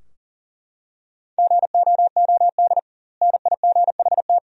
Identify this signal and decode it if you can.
Morse code, heard as a keyed tone, about 39 wpm, 700 Hz. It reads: GOOD NIGHT
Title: 2x Word-Spacing - Sentences from Top 300 Words - 40wpm